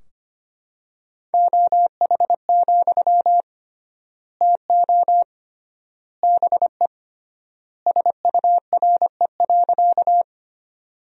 Morse code, heard as a keyed tone, about 25 wpm, 700 Hz. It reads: OH, TO BE SURE.
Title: 3x Word-Spacing - Sentences from Top 500 Words - 25wpm